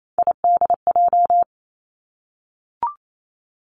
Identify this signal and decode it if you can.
Morse code, heard as a keyed tone, about 28 wpm, 700 Hz. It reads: IDJ
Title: Three Challenging Letter-Number Combinations 28wpm